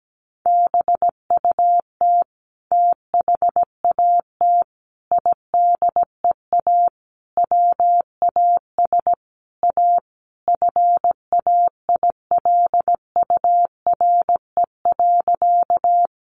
Morse code, heard as a keyed tone, about 17 wpm, 700 Hz. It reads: BUT THAT IDEA WAS A FAILURE.
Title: Sentences from Top 900 Words 17wpm